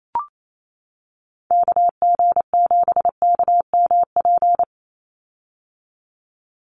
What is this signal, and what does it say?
Morse code, heard as a keyed tone, about 28 wpm, 700 Hz. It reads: KG7KMP
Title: USA Call Signs 28wpm